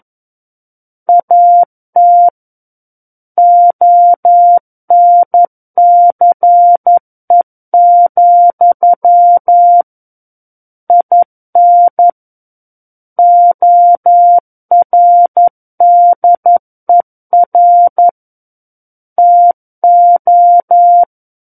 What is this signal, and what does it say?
Morse code, heard as a keyed tone, about 11 wpm, 700 Hz. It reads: AT ONCE, IN ORDER TO
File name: Book One / 02-Chapter 2 - 11 WPM